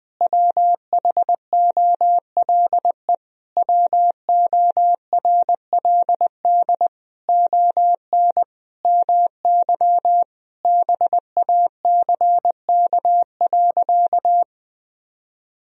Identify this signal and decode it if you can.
Morse code, heard as a keyed tone, about 20 wpm, 700 Hz. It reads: WHOLE WORLD ON MY BACK.
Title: Sentences from Top 1000 Words - Review Part 1 - 20wpm